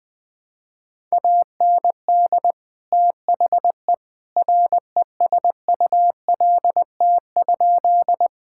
Morse code, heard as a keyed tone, about 20 wpm, 700 Hz. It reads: AND THE RESULT?
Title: Sentences from Top 1000 Words - Review Part 5 - 20wpm